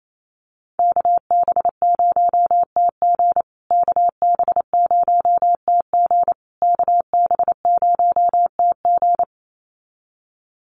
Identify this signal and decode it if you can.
Morse code, heard as a keyed tone, about 28 wpm, 700 Hz. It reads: KB0TG KB0TG KB0TG
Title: USA Call Signs Repeated 3x 28wpm